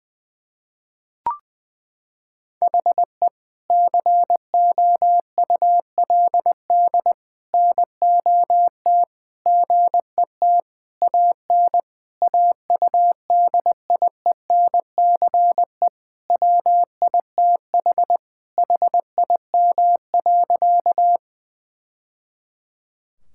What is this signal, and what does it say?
Morse code, heard as a keyed tone, about 20 wpm, 700 Hz. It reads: HE COULD NOT GET AN AUDIENCE WITH HIM.
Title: Sentences from Top 1000 Words - Review Part 5 - 20wpm